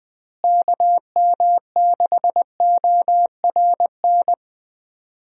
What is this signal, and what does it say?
Morse code, heard as a keyed tone, about 21 wpm, 700 Hz. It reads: KM6ORN
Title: USA Call Signs - 6 Characters - Repeated 2x - 20wpm